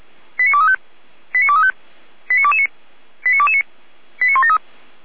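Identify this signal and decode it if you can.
Five-tone selective call (ZVEI), heard as quick runs of five tones, tones of about 70 ms. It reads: 89235 89235 89229 89229 89173
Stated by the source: ZVEI five-tone call received by SDR, from the audio server of Feuerwehr Orlamünde (fire brigade).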